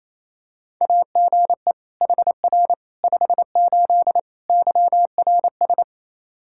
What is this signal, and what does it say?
Morse code, heard as a keyed tone, about 27 wpm, 700 Hz. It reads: AGE HR 58 YRS
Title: Practice QSO - Exchange 3 - More Info - 28wpm